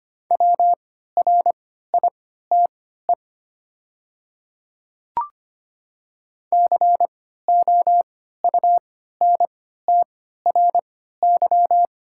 Morse code, heard as a keyed tone, about 25 wpm, spaced as for 14 wpm, 690 Hz. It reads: WRITE COUNTRY
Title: 3x Character Spacing - Top 500 Words - 25wpm